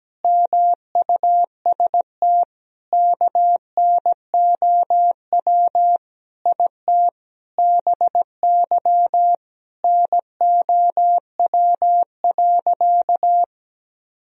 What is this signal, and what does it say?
Morse code, heard as a keyed tone, about 17 wpm, 700 Hz. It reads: MUST KNOW IT BY NOW.
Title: Sentences from Top 1000 Words - Review Part 2 - 17wpm